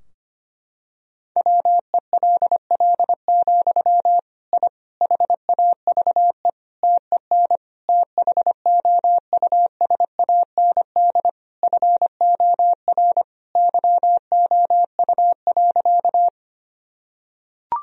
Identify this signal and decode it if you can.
Morse code, heard as a keyed tone, about 25 wpm, 700 Hz. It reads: WELL, I HAVE TEN THOUSAND FOR YOU.
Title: Sentences from Top 1000 Words - Review Part 3 - 25wpm